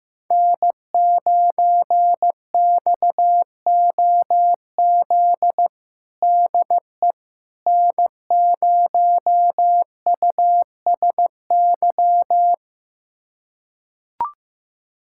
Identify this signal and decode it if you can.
Morse code, heard as a keyed tone, about 15 wpm, 700 Hz. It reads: N9XOZ DE N0USY
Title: USA Callsign DE Callsign - 5 Characters - 15wpm